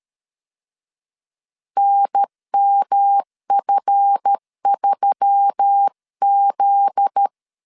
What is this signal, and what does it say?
Morse code, heard as a keyed tone, about 13 wpm, 800 Hz. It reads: NMF3Z
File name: LEZIONI / 9 2867 12wpm 2322